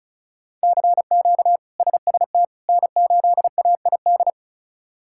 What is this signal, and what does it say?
Morse code, heard as a keyed tone, about 34 wpm, 700 Hz. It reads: CQ SST N8AID
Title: K1USN SST QSO Practice - 35wpm